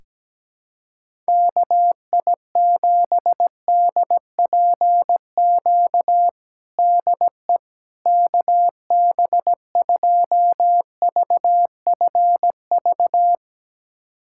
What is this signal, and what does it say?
Morse code, heard as a keyed tone, about 17 wpm, 700 Hz. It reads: KI7DPQ DE KB2VFV
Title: USA Callsign DE Callsign - 6 Characters - 17wpm